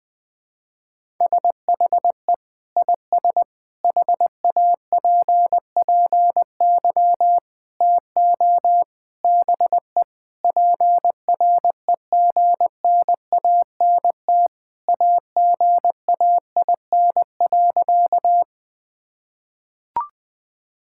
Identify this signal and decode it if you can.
Morse code, heard as a keyed tone, about 21 wpm, 700 Hz. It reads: SHE IS HAPPY TO BE PREGNANT AGAIN.
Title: Sentences from Top 1000 Words - Review Part 6 - 20wpm